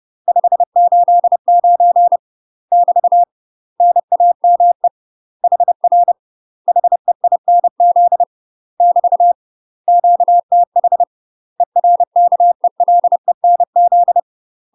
Morse code, heard as a keyed tone, about 30 wpm, 700 Hz. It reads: 589 <BT> NAME HR HEINZ <BT> QTH ERKELENZ
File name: UK MASTER QSOs TRAINING / 30WPM / UK MASTER QSOs 0014